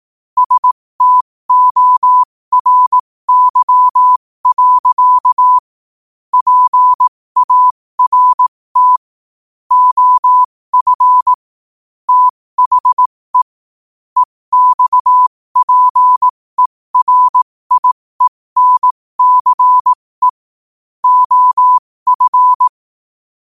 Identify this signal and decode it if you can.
Morse code, heard as a keyed tone, about 18 wpm, spaced as for 15 wpm, 1000 Hz. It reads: STORY. PART OF THE EXPERIENCE OF
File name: qotdfiles / 15wpm